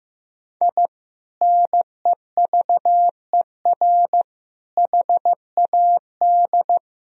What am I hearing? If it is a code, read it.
Morse code, heard as a keyed tone, about 16 wpm, 700 Hz. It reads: I NEVER HAD
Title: Sentences from Top 1000 Words - Review Part 4 - 15wpm